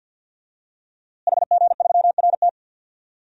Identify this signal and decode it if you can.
Morse code, heard as a keyed tone, about 52 wpm, 700 Hz. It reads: HG3RT
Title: DXCC-100 Call Signs - 5 Characters - Repeated 3x - 50wpm